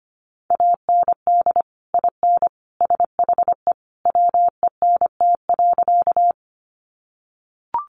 Morse code, heard as a keyed tone, about 25 wpm, 700 Hz. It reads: AND IN SHE WENT.
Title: Sentences from Top 200 Words 25wpm